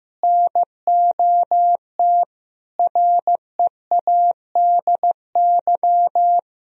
Morse code, heard as a keyed tone, about 16 wpm, 700 Hz. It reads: NOT READY
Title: Sentences from Top 1000 Words - Review Part 2 - 15wpm